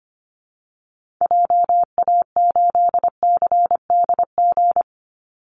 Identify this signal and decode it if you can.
Morse code, heard as a keyed tone, about 25 wpm, 700 Hz. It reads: JA8CDG
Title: Call Signs Repeated 3x - 25wpm